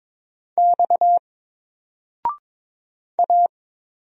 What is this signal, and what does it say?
Morse code, heard as a keyed tone, about 22 wpm, 700 Hz. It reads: X A
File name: Single-Letter-Number-22wpm